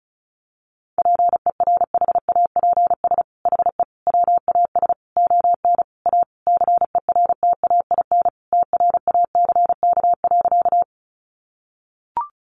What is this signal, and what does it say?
Morse code, heard as a keyed tone, about 34 wpm, 700 Hz. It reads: PERHAPS HE WAS ON A CERTAIN TRACK.
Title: Sentences from Top 1100 Words 35wpm